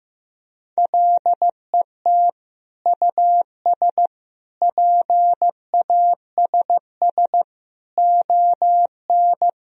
Morse code, heard as a keyed tone, about 15 wpm, 700 Hz. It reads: LET US PASS ON
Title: Sentences from Top 1000 Words - Review Part 1 - 15wpm